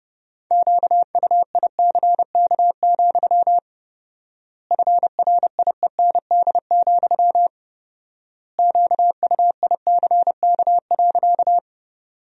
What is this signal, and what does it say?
Morse code, heard as a keyed tone, about 30 wpm, 700 Hz. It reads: QUICK, FRIEND, QUICK.
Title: 4x Word-Spacing - Sentences from Top 500 Words - 30wpm